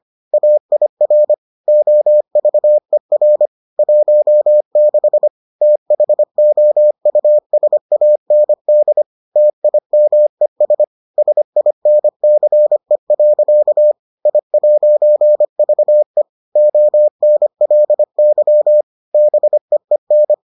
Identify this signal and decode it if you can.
Morse code, heard as a keyed tone, about 25 wpm, 600 Hz. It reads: AIR OVER 16 THOUSAND TIMES SINCE. I'VE ONLY BEEN